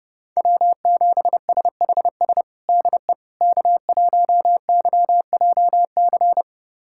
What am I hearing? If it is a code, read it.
Morse code, heard as a keyed tone, about 30 wpm, 700 Hz. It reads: W7SHS DE K1YJC
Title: USA Callsign DE Callsign - 5 Characters - 30wpm